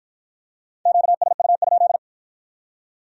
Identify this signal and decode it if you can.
Morse code, heard as a keyed tone, about 52 wpm, 700 Hz. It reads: QSU?